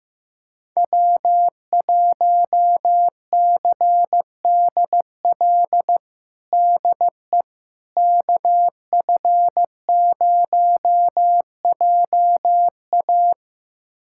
Morse code, heard as a keyed tone, about 15 wpm, 700 Hz. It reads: W1CDL DE KF0JA
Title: USA Callsign DE Callsign - 5 Characters - 15wpm